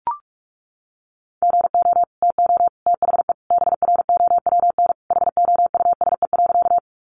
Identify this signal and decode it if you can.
Morse code, heard as a keyed tone, about 44 wpm, 700 Hz. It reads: GO TO THE BROWN HOUSE.
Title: Sentences from Top 1200 Words 45wpm